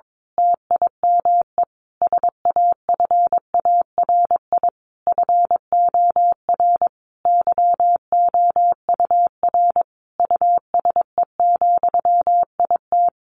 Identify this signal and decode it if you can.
Morse code, heard as a keyed tone, about 22 wpm, 700 Hz. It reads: TIME SAFARI FOR YOUR USE, IT
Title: A Sound Of Thunder 22wpm